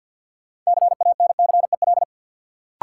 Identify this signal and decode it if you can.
Morse code, heard as a keyed tone, about 48 wpm, 700 Hz. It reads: CANCEL
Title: Sentences from Top 1200 Words - New Word Drill Alt - 50wpm